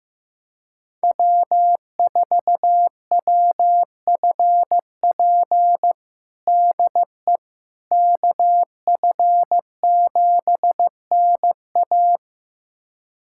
Morse code, heard as a keyed tone, about 16 wpm, 700 Hz. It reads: W4WFP DE KF7NA
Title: USA Callsign DE Callsign - 5 Characters - 15wpm